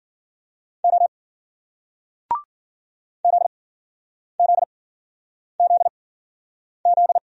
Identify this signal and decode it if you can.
Morse code, heard as a keyed tone, about 55 wpm, spaced as for 9 wpm, 700 Hz. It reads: O ZZZZ